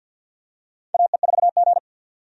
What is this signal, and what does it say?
Morse code, heard as a keyed tone, about 53 wpm, 700 Hz. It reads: AE4G
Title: USA Call Signs - 4 Characters - 50wpm